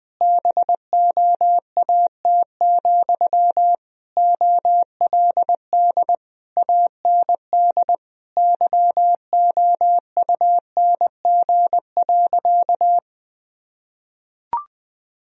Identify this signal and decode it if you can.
Morse code, heard as a keyed tone, about 20 wpm, 700 Hz. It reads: BOAT, OLD AND YOUNG.
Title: Sentences from Top 1000 Words - Review Part 3 - 20wpm